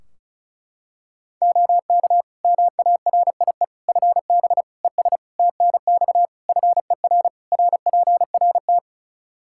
Morse code, heard as a keyed tone, about 36 wpm, 700 Hz. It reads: OK MARIE FB ES TNX FER RPRT
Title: Practice QSO - Exchange 2 - Info - 35wpm